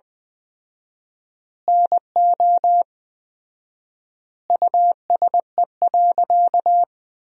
Morse code, heard as a keyed tone, about 20 wpm, 700 Hz. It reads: NO USE.
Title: 4x Word-Spacing - Sentences from Top 100 Words - 20wpm